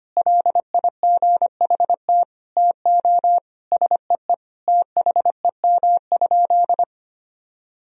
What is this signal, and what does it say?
Morse code, heard as a keyed tone, about 25 wpm, 700 Hz. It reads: LIGHT TO SEE THEM?
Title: Sentences from Top 1000 Words - Review Part 1 - 25wpm